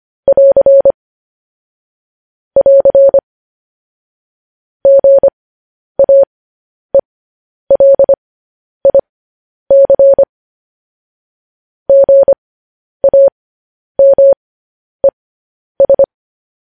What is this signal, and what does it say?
Morse code, heard as a keyed tone, about 25 wpm, spaced as for 10 wpm, 550 Hz. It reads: <AR> <AR> GAELIC GAMES
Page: news-headlines-in-morse-code-at-10-wpm